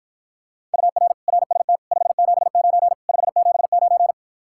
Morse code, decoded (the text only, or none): UR RST 579 579